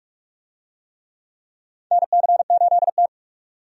Morse code, heard as a keyed tone, about 48 wpm, 700 Hz. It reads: NC8T
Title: USA Call Signs - 4 Characters - 45wpm